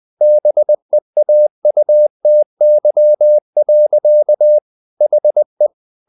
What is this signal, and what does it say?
Morse code, heard as a keyed tone, about 20 wpm, 600 Hz. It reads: BEAUTY. HE
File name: the old man and the sea part 2 20wpm